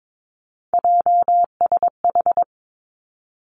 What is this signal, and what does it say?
Morse code, heard as a keyed tone, about 22 wpm, 700 Hz. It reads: JSH
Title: Three Challenging Letter-Number Combinations 22wpm